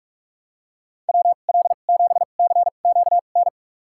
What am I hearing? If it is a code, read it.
Morse code, heard as a keyed tone, about 47 wpm, spaced as for 33 wpm, 700 Hz. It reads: WL7CQN
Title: USA Call Signs - 6 Characters - 2x Character Spacing - 45wpm